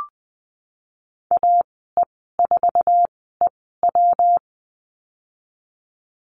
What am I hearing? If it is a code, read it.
Morse code, heard as a keyed tone, about 20 wpm, spaced as for 15 wpm, 700 Hz. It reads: AE4EW
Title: 154 Lesson 25g - Letter p - Call Signs 20wpm